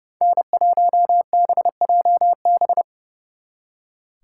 Morse code, heard as a keyed tone, about 35 wpm, 700 Hz. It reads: N1BJB
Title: Speed Racing - USA 1x3 Call Signs 30wpm